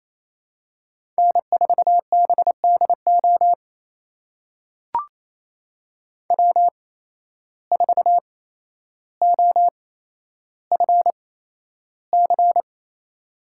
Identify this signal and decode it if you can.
Morse code, heard as a keyed tone, about 28 wpm, spaced as for 12 wpm, 700 Hz. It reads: N4BDO W4OFC
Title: USA Call Signs - 5 Characters - 8x Character Spacing - 28wpm